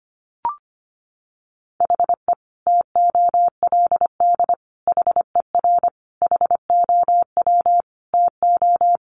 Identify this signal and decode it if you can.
Morse code, heard as a keyed tone, about 25 wpm, 700 Hz. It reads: HE TOLD HER HOW TO
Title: Sentences from Top 1500 Words 25wpm